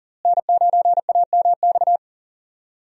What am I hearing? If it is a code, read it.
Morse code, heard as a keyed tone, about 40 wpm, 690 Hz. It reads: N9AMX
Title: USA Call Signs - 5 Characters - Repeated 3x - 40wpm